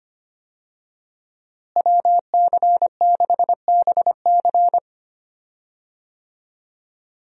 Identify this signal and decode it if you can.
Morse code, heard as a keyed tone, about 25 wpm, 700 Hz. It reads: WC6BC
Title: USA Call Signs - 5 Characters - Repeated 3x - 25wpm